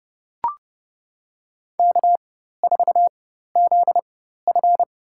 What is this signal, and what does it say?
Morse code, heard as a keyed tone, about 30 wpm, spaced as for 14 wpm, 700 Hz. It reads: K4ZF
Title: USA Call Signs - 4 Characters - 4x Character Spacing - 30wpm